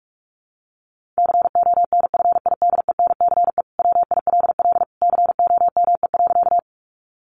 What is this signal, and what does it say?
Morse code, heard as a keyed tone, about 44 wpm, 700 Hz. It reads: CONFIDENCE WILL COME.
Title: Sentences from Top 1500 Words 45wpm